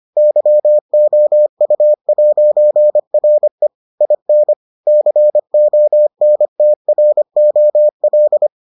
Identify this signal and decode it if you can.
Morse code, heard as a keyed tone, about 25 wpm, 600 Hz. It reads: YOU'RE IN CONTROL